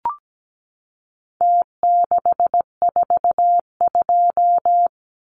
Morse code, heard as a keyed tone, about 17 wpm, 700 Hz. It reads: T642
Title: Numbers - Contest Style - 4-Digit 17wpm